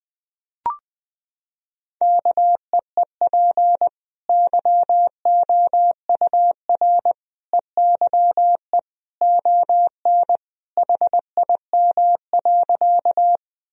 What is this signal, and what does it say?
Morse code, heard as a keyed tone, about 20 wpm, 700 Hz. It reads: KEEP YOUR EYE ON HIM.
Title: Sentences from Top 1000 Words - Review Part 1 - 20wpm